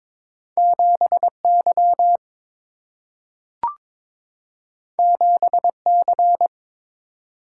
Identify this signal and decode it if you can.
Morse code, heard as a keyed tone, about 22 wpm, 700 Hz. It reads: 7 Y 7 C
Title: Single-Letter Single-Number 22wpm